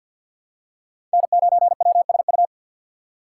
Morse code, heard as a keyed tone, about 53 wpm, 700 Hz. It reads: N9WSU
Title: USA Call Signs - 5 Characters - 3x Character Spacing - 50wpm